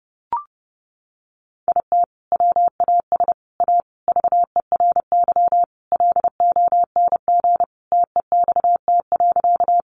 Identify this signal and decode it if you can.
Morse code, heard as a keyed tone, about 30 wpm, 700 Hz. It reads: IT WAS A VERY LONG TEXT.
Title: Sentences from Top 1200 Words 30wpm